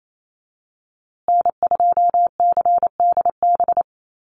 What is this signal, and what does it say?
Morse code, heard as a keyed tone, about 28 wpm, 700 Hz. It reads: N2CDB
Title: USA Call Signs Repeated 2x 28wpm